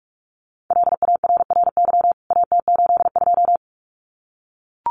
Morse code, heard as a keyed tone, about 44 wpm, 700 Hz. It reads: LARRY AT82
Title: CWT Exchange 45wpm